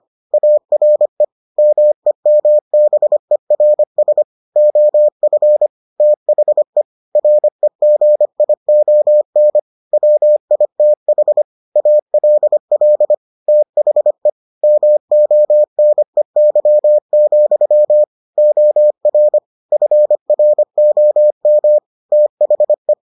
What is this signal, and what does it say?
Morse code, heard as a keyed tone, about 25 wpm, 600 Hz. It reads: ARE MEMBERS OF THE REGION WITH ALL THE MONEY, OR FROM THE